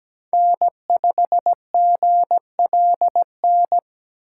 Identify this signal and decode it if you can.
Morse code, heard as a keyed tone, about 17 wpm, 700 Hz. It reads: N5GLN
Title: USA Call Signs - 5 Characters - 8x Character Spacing - 17wpm